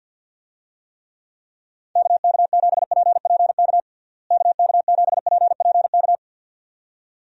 Morse code, heard as a keyed tone, about 47 wpm, 700 Hz. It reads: KK7PPK KK7PPK
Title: USA Call Signs - 6 Characters - Repeated 2x - 50wpm